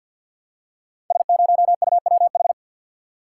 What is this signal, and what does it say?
Morse code, heard as a keyed tone, about 53 wpm, 700 Hz. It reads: S0UWH
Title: Code Groups - 5 Letter:Numbers - 50wpm